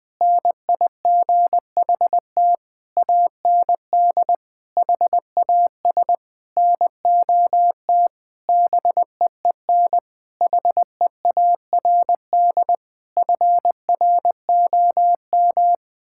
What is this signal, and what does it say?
Morse code, heard as a keyed tone, about 20 wpm, 700 Hz. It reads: NIGHT AND HAS NOT BEEN HEARD FROM
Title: Sentences from Top 1600 Words 20wpm